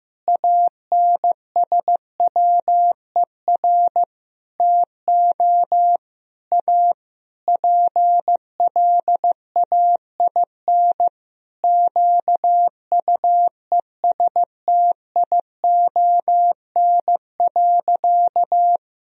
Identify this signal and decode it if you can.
Morse code, heard as a keyed tone, about 15 wpm, 700 Hz. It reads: ANSWER TO A PLAIN QUESTION.
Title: Sentences from Top 1000 Words - Review Part 2 - 15wpm